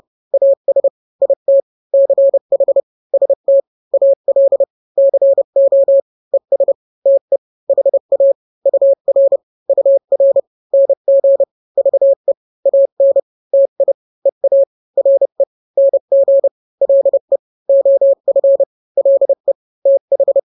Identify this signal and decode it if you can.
Morse code, heard as a keyed tone, about 30 wpm, 550 Hz. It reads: AS IT CH ST AL CO ES TE HA UR UR NG VE AN TI EA RE NG LE OF LE TH
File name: n-grams / 2-grams 30 25 WPM